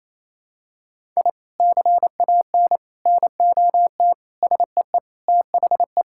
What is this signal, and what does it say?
Morse code, heard as a keyed tone, about 28 wpm, 700 Hz. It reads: I CAN NOT SEE THE
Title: Sentences from Top 1000 Words - Review Part 6 - 28wpm